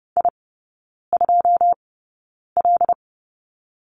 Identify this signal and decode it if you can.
Morse code, heard as a keyed tone, about 30 wpm, spaced as for 9 wpm, 700 Hz. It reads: I2L
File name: 081 Lesson 14e - Number 2 - Three Characters 30wpm